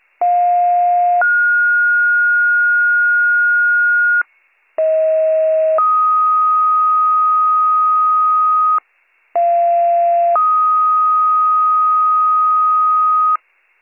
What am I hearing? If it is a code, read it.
Two-tone page, heard as a one-second tone, then a three-second tone, about 710 then 1430 Hz, about 640 then 1250 Hz, about 690 then 1250 Hz. It is heard from 0.2 to 13.4 s.